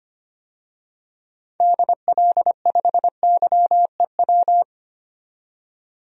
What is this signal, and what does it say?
Morse code, heard as a keyed tone, about 25 wpm, 700 Hz. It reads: DL5YEW
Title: DXCC-100 Call Signs - 6 Characters - 25wpm